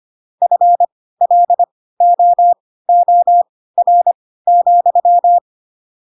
Morse code, heard as a keyed tone, about 25 wpm, spaced as for 16 wpm, 700 Hz. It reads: FLOOR,
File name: Pinocchio / Pinocchio ch7 10 25 ik1hge